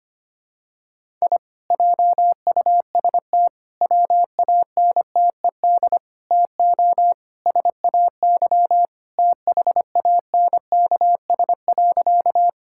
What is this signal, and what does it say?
Morse code, heard as a keyed tone, about 25 wpm, 700 Hz. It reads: I JUST WANTED TO SAY THANKS.
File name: practice / 25wpm / Sentences with 4-Word Phrases - 25wpm